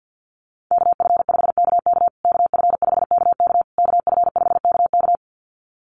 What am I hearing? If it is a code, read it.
Morse code, heard as a keyed tone, about 50 wpm, 700 Hz. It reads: KF5KK KF5KK KF5KK
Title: USA Call Signs Repeated 3x 50wpm